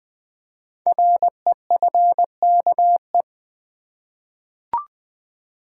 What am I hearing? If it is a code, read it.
Morse code, heard as a keyed tone, about 20 wpm, 700 Hz. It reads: REFKE
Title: Code Groups - 5 Letter:Numbers - 20wpm